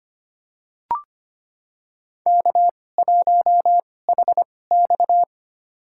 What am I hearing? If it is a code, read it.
Morse code, heard as a keyed tone, about 25 wpm, spaced as for 18 wpm, 700 Hz. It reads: K1HX
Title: USA Call Signs - 4 Characters - 2x Character Spacing - 25wpm